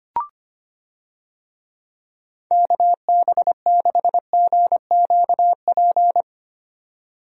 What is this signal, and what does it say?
Morse code, heard as a keyed tone, about 25 wpm, 700 Hz. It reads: KB6GQP
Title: USA Call Signs - 6 Characters - 25wpm